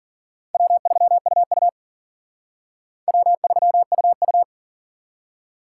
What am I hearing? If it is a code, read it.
Morse code, heard as a keyed tone, about 47 wpm, 700 Hz. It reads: W3UU W3UU
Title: Speed Racing - USA 1x2 Call Signs 40wpm